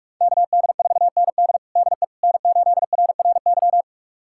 Morse code, heard as a keyed tone, about 41 wpm, 700 Hz. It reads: KD4ND DE N8RRY
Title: USA Callsign DE Callsign - 5 Characters - 45wpm